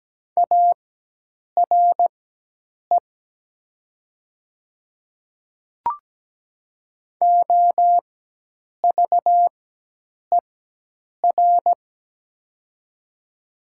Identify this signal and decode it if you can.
Morse code, heard as a keyed tone, about 17 wpm, spaced as for 8 wpm, 700 Hz. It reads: ARE OVER
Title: 4x Character Spacing - Top 100 Words - 17wpm